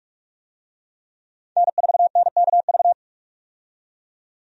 Morse code, heard as a keyed tone, about 46 wpm, 700 Hz. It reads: N4NKV
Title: USA Call Signs - 5 Characters - 45wpm